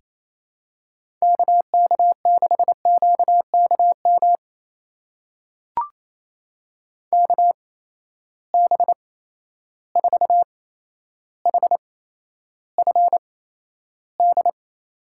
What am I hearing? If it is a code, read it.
Morse code, heard as a keyed tone, about 27 wpm, spaced as for 12 wpm, 700 Hz. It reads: KK6QKM KB4HFD
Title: USA Call Signs - 6 Characters - 8x Character Spacing - 28wpm